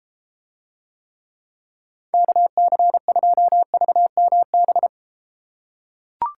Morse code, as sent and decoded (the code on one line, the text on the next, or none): -.- -.-. ..--- ...- -- -...
KC2VMB